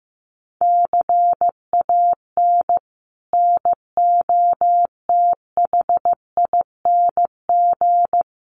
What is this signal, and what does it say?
Morse code, heard as a keyed tone, about 15 wpm, 700 Hz. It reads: CAN NOTHING
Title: Sentences from Top 1100 Words 15wpm